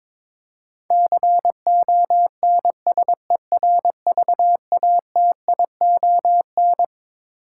Morse code, heard as a keyed tone, about 21 wpm, 700 Hz. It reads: CONSERVATION
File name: Twelve Letter Words - 22wpm